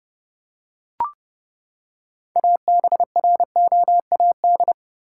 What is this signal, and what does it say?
Morse code, heard as a keyed tone, about 31 wpm, 700 Hz. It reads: ABROAD
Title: Six Letter Words - 30wpm